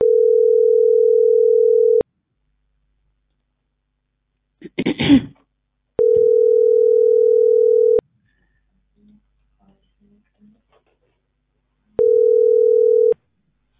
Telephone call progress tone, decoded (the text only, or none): ringback tone